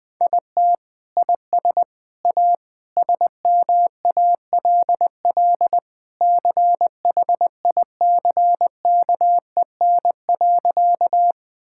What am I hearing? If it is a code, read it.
Morse code, heard as a keyed tone, about 20 wpm, 700 Hz. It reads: IT IS A SMALL CHICKEN.
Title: Sentences from Top 1000 Words - Review Part 5 - 20wpm